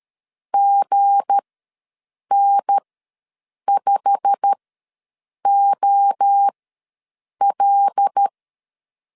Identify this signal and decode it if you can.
Morse code, heard as a keyed tone, about 13 wpm, spaced as for 7 wpm, 800 Hz. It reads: GN5OL
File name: LEZIONI / 9 2 ws 12wpm 3406